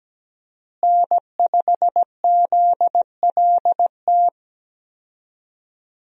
Morse code, heard as a keyed tone, about 17 wpm, 700 Hz. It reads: N5ZLT
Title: USA Call Signs - 5 Characters - Repeated 2x - 17wpm